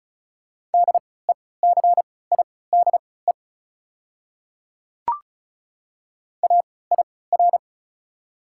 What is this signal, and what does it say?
Morse code, heard as a keyed tone, about 36 wpm, spaced as for 20 wpm, 690 Hz. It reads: DECIDE AIR
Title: 3x Character Spacing - Top 500 Words - 35wpm